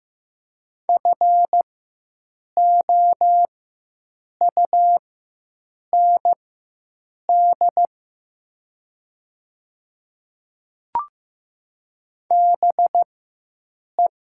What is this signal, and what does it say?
Morse code, heard as a keyed tone, about 15 wpm, spaced as for 7 wpm, 700 Hz. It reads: FOUND BE
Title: 4x Character Spacing - Top 500 Words - 15wpm